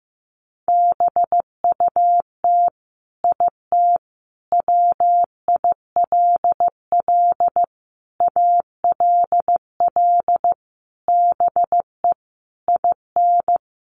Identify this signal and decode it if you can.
Morse code, heard as a keyed tone, about 15 wpm, 700 Hz. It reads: BUT IT WILL ALL BE IN
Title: Sentences from Top 1400 Words 15wpm